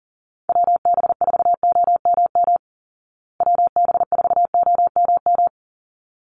Morse code, heard as a keyed tone, about 40 wpm, 700 Hz. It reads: WB4OMM WB4OMM
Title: Call Signs Repeated 3x - 40wpm